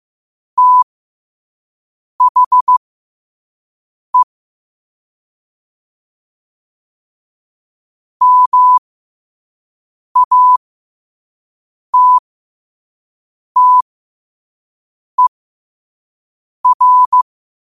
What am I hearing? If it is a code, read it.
Morse code, heard as a keyed tone, about 15 wpm, spaced as for 5 wpm, 1000 Hz. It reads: THE MATTER
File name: qotdfiles / 5wpm